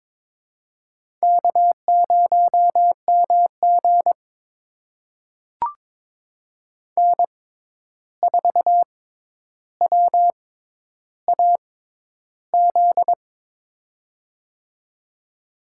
Morse code, heard as a keyed tone, about 22 wpm, spaced as for 11 wpm, 700 Hz. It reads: K0MG N4WAZ
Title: USA Call Signs - 6x Character Spacing - 22wpm